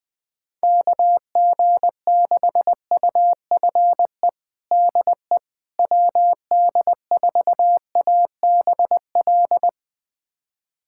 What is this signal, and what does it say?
Morse code, heard as a keyed tone, about 21 wpm, 700 Hz. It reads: KG6UFE DE WD4ABL
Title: USA Callsign DE Callsign 20wpm